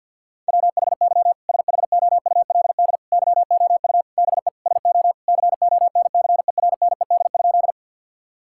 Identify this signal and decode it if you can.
Morse code, heard as a keyed tone, about 50 wpm, 700 Hz. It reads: WHY SHOULD YOU BE SO CONCERNED?
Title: Sentences from Top 1000 Words - Review Part 6 - 50wpm